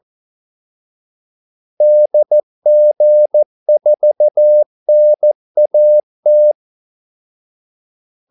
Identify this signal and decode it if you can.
Morse code, heard as a keyed tone, about 14 wpm, 600 Hz. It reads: DG4NAT
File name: Callsigns-14wpm